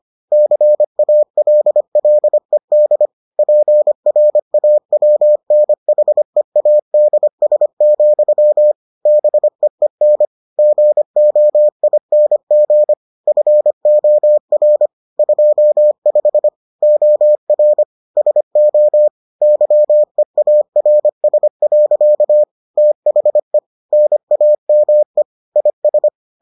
Morse code, heard as a keyed tone, about 25 wpm, 600 Hz. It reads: CALLED PRAWNHEADS, BEEN GOING FOR 25 OR SO YEARS. THE NAME IS